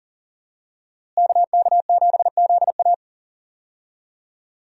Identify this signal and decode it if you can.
Morse code, heard as a keyed tone, about 40 wpm, 700 Hz. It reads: KK7ZA
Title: USA Call Signs - 5 Characters - Repeated 3x - 40wpm